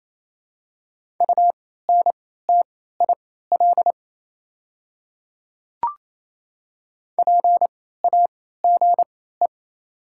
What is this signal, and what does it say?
Morse code, heard as a keyed tone, about 28 wpm, spaced as for 16 wpm, 700 Hz. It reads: UNTIL PAGE
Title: 3x Character Spacing - Top 300 Words - 28wpm